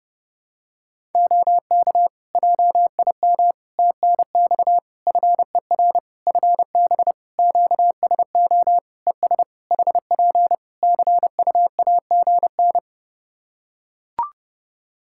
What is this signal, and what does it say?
Morse code, heard as a keyed tone, about 30 wpm, 700 Hz. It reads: OK JIM TNX FER FB QSO ES HP CUAGN
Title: Practice QSO - Exchange 4 - Ending - 30wpm